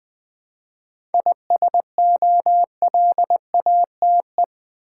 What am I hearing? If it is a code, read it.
Morse code, heard as a keyed tone, about 20 wpm, 700 Hz. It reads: ISOLATE
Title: 7 Letter Words - First 3 Letters Given - 20wpm